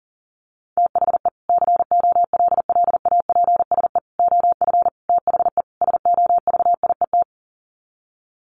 Morse code, heard as a keyed tone, about 40 wpm, 700 Hz. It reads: THE COLLAPSE OF THE SOVIET
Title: Sets of 5 Words Encore 40wpm